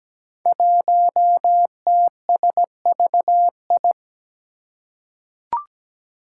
Morse code, heard as a keyed tone, about 17 wpm, 700 Hz. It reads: 1TSVI
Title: Five Challenging Letter-Number Combinations 17wpm